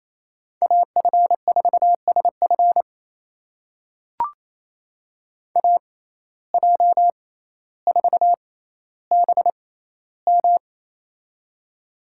Morse code, heard as a keyed tone, about 28 wpm, spaced as for 14 wpm, 700 Hz. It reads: AF4SF AJ4BM
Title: USA Call Signs - 5 Characters - 6x Character Spacing - 28wpm